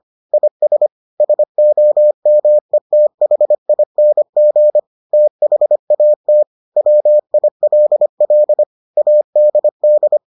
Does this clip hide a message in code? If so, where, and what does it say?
Yes: Morse code from 0.3 to 10.2 s, heard as a keyed tone, about 25 wpm, 600 Hz. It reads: IS SOMETHING THAT WILL ADD